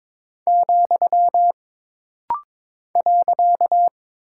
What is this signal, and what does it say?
Morse code, heard as a keyed tone, about 22 wpm, 700 Hz. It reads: , .
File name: practice / 22wpm / Single Punctuation-Prosign - Rapid-Fire - 22wpm